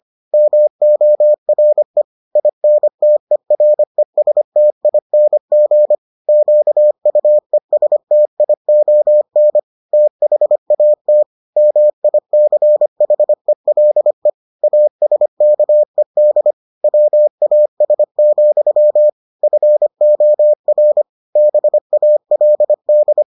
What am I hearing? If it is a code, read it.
Morse code, heard as a keyed tone, about 25 wpm, 600 Hz. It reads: MORE INTERESTING QUESTION THAT MICHELE ASKED WAS, FOR BALD